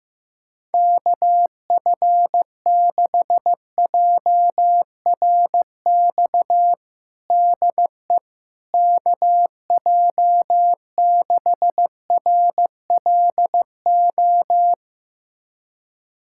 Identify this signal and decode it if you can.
Morse code, heard as a keyed tone, about 15 wpm, 700 Hz. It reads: KF6JRX DE KJ6RLO
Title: USA Callsign DE Callsign 15wpm